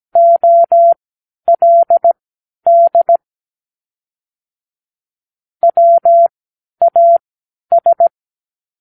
Morse code, heard as a keyed tone, about 17 wpm, spaced as for 10 wpm, 680 Hz. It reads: OLD WAS
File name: k7qo / Practicewords